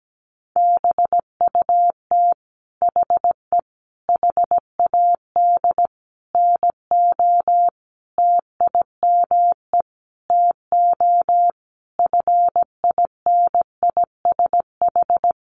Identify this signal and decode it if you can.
Morse code, heard as a keyed tone, about 17 wpm, 700 Hz. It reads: BUT HE HAD NO TIME TO FINISH
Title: Sentences from Top 1200 Words 17wpm